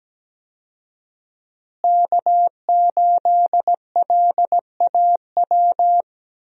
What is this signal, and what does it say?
Morse code, heard as a keyed tone, about 17 wpm, 700 Hz. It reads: K8LAW
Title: USA Call Signs - 5 Characters - Repeated 3x - 17wpm